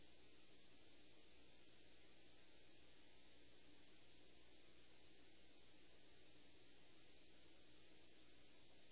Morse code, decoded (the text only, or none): none